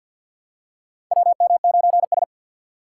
Morse code, heard as a keyed tone, about 48 wpm, 700 Hz. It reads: WM9S
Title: USA Call Signs - 4 Characters - 6x Character Spacing - 50wpm